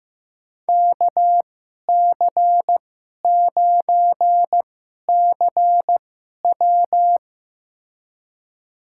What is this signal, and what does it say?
Morse code, heard as a keyed tone, about 15 wpm, spaced as for 11 wpm, 700 Hz. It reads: KC9CW
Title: USA Call Signs - 2x Character Spacing - 15wpm